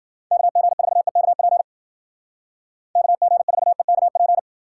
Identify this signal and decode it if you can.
Morse code, heard as a keyed tone, about 52 wpm, 700 Hz. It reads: KG4EKP KG4EKP